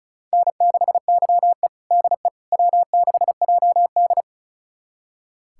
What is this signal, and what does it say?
Morse code, heard as a keyed tone, about 36 wpm, 700 Hz. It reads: N6YE DE W6JD
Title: USA Callsign DE Callsign - 4 Characters - 35wpm